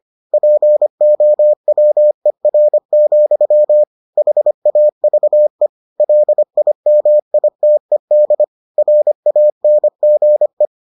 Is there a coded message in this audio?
Yes: Morse code from 0.3 to 10.7 s, heard as a keyed tone, about 25 wpm, 600 Hz. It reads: POWER, HAVE LIMITED RANGE